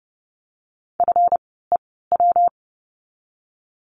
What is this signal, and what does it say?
Morse code, heard as a keyed tone, about 30 wpm, spaced as for 17 wpm, 700 Hz. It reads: FEW
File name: course / 30:10wpm / 237 Lesson 38f - Letter q - Words 30wpm